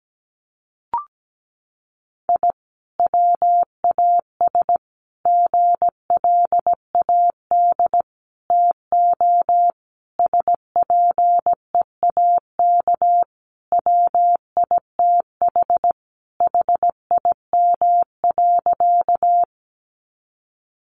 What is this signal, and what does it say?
Morse code, heard as a keyed tone, about 17 wpm, 700 Hz. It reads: I WAS GLAD TO SPEAK WITH HIM.
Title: Sentences from Top 1100 Words 17wpm